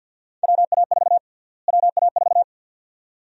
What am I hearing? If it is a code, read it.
Morse code, heard as a keyed tone, about 52 wpm, 700 Hz. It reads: WA4 WA4
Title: USA Call Sign Prefixes - Repeated 2x - 50wpm